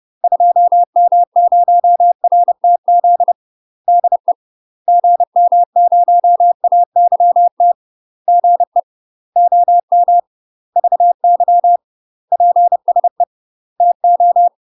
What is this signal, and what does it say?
Morse code, heard as a keyed tone, about 30 wpm, 700 Hz. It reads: 2M0RTZ DE GM0AYT GE OM VY PSE TO
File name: UK MASTER QSOs TRAINING / 30WPM / UK MASTER QSOs 0002